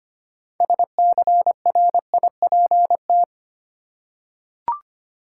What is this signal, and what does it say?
Morse code, heard as a keyed tone, about 25 wpm, 700 Hz. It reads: SCRIPT
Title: Six Letter Words - 25wpm